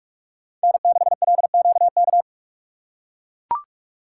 Morse code, heard as a keyed tone, about 43 wpm, 700 Hz. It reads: N6LQK
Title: USA Call Signs - 5 Characters - 6x Character Spacing - 45wpm